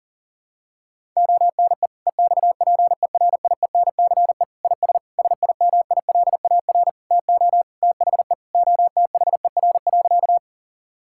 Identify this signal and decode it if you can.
Morse code, heard as a keyed tone, about 37 wpm, 700 Hz. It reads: ONE EXPERIENCE IS SIMILAR TO THE OTHER.
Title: Sentences from Top 1000 Words - Review Part 5 - 40wpm